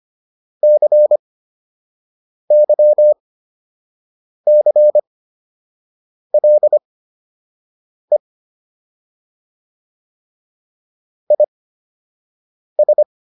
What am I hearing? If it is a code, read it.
Morse code, heard as a keyed tone, about 25 wpm, spaced as for 6 wpm, 600 Hz. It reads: CYCLE IS